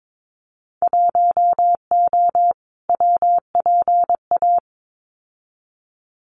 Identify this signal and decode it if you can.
Morse code, heard as a keyed tone, about 22 wpm, 700 Hz. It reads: 1O WPA
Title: Winter Field Day Exchanges 22wpm